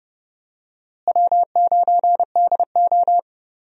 Morse code, heard as a keyed tone, about 31 wpm, 700 Hz. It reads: W9DO
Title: USA Call Signs - 4 Characters - Repeated 3x - 30wpm